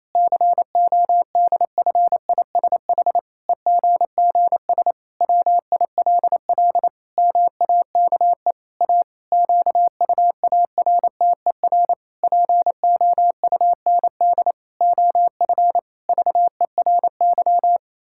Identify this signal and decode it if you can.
Morse code, heard as a keyed tone, about 28 wpm, 700 Hz. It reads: CODFISH EGGS WILL MAKE A QUARTER POUND OF VERY
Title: Pangram Challenge v2 - 28wpm